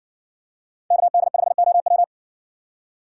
Morse code, heard as a keyed tone, about 60 wpm, 700 Hz. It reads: KD5OW